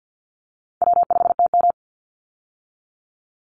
Morse code, heard as a keyed tone, about 50 wpm, 700 Hz. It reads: W5TM
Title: Call Signs Repeated 2x - 50wpm